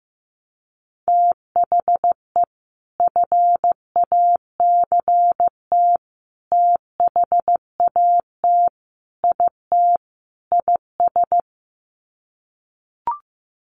Morse code, heard as a keyed tone, about 15 wpm, 700 Hz. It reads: THE FACT THAT IT IS
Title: Sets of 5 Words Encore2 15wpm